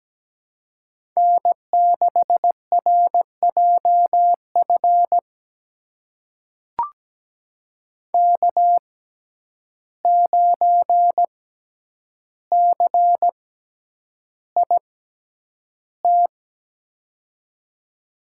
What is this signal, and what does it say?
Morse code, heard as a keyed tone, about 17 wpm, spaced as for 9 wpm, 700 Hz. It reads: N6RJF K9CIT
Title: USA Call Signs - 5 Characters - 6x Character Spacing - 17wpm